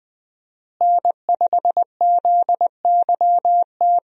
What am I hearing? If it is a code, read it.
Morse code, heard as a keyed tone, about 21 wpm, 700 Hz. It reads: N5ZYT
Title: USA Call Signs - 5 Characters - Repeated 2x - 20wpm